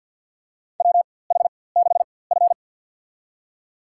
Morse code, heard as a keyed tone, about 51 wpm, spaced as for 23 wpm, 700 Hz. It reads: WH6F
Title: USA Call Signs - 4x Character Spacing - 50wpm